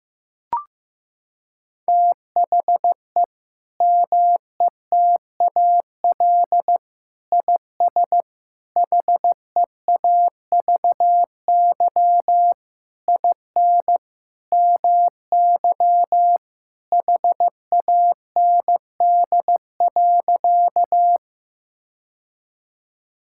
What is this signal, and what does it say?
Morse code, heard as a keyed tone, about 15 wpm, 700 Hz. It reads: THE METAL IS HEAVY IN MY HAND.
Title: Sentences from Top 1000 Words - Review Part 6 - 15wpm